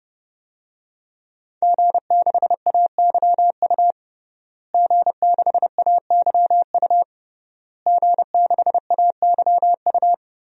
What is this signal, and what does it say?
Morse code, heard as a keyed tone, about 30 wpm, 700 Hz. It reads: G6AYU G6AYU G6AYU
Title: DXCC-100 Call Signs - 5 Characters - Repeated 3x - 30wpm